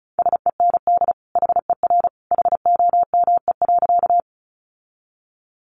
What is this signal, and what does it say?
Morse code, heard as a keyed tone, about 34 wpm, 700 Hz. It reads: SEND HER HOME.
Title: Sentences from Top 1500 Words 35wpm